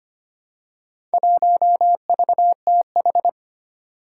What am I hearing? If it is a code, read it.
Morse code, heard as a keyed tone, about 25 wpm, 700 Hz. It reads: 1VTH
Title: Four Challenging Letter-Number Combinations 25wpm